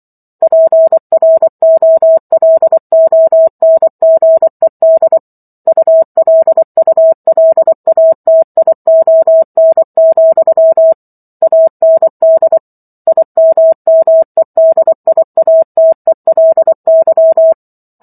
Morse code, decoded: PROLONGED ULULATION, AND IMMEDIATELY

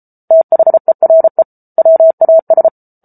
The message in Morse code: THERE WAS

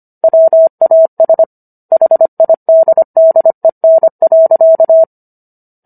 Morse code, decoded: WAS HIDDEN.

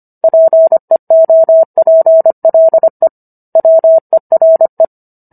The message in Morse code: PEOPLE WERE

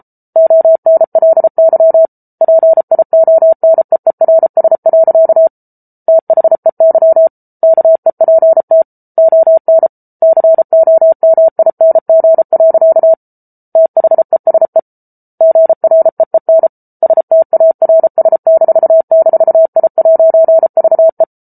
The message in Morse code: ONLY PIONEERS. THEY KEPT ON COMING. THESE GREEN STARS--I'VE

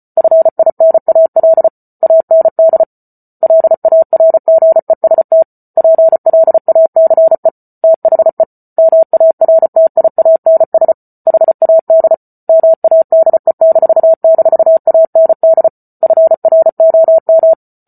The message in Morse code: FINAL AND LARGEST PLACE THE MARTIANS HAD MADE--AND FROM

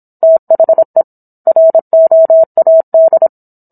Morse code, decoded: THE ROAD

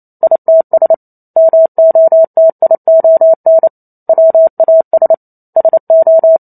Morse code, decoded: ITS MOTION WAS SO

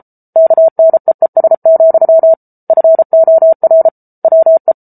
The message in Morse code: KNEES, FOR WE

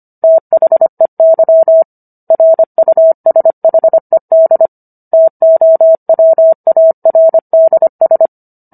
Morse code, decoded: THEY RUSHED TOWARDS

